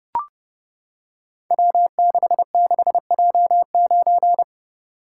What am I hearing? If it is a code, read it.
Morse code, heard as a keyed tone, about 30 wpm, 700 Hz. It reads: W66J9
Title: Code Groups - 5 Letter:Numbers - 30wpm